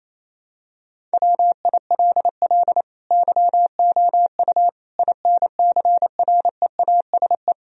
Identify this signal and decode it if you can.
Morse code, heard as a keyed tone, about 28 wpm, 700 Hz. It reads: WILL YOU INCREASE
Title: Sentences from Top 1000 Words - Review Part 4 - 28wpm